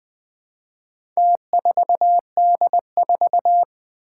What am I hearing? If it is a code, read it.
Morse code, heard as a keyed tone, about 21 wpm, 700 Hz. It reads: T4D4
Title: Four Challenging Letter-Number Combinations 20wpm